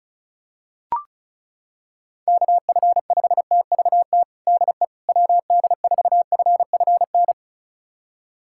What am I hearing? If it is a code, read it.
Morse code, heard as a keyed tone, about 36 wpm, 690 Hz. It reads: KF5TVT DE WD4FFN
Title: USA Callsign DE Callsign - 6 Characters - 35wpm